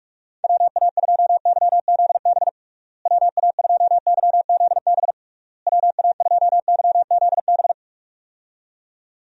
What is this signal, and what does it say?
Morse code, heard as a keyed tone, about 46 wpm, 700 Hz. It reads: WA2YZB WA2YZB WA2YZB
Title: USA Call Signs - 6 Characters - Repeated 3x - 45wpm